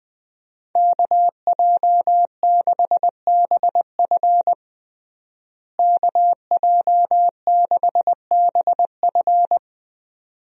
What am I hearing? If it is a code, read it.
Morse code, heard as a keyed tone, about 21 wpm, 700 Hz. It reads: KJ6BF KJ6BF
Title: USA Call Signs - 5 Characters - Repeated 2x - 20wpm